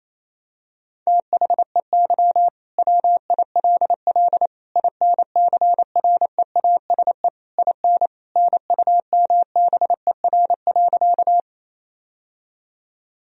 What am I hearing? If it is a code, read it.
Morse code, heard as a keyed tone, about 28 wpm, 700 Hz. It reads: THEY WILL INCREASE IN NUMBER.
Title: Sentences from Top 1000 Words - Review Part 4 - 28wpm